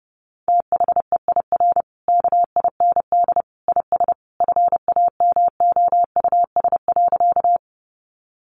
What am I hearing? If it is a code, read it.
Morse code, heard as a keyed tone, about 28 wpm, 700 Hz. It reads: THEIR KIND IS FAMOUS.
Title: Sentences from Top 1500 Words 30wpm